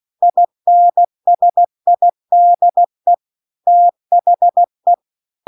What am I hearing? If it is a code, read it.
Morse code, heard as a keyed tone, about 16 wpm, 700 Hz. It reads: INSIDE THE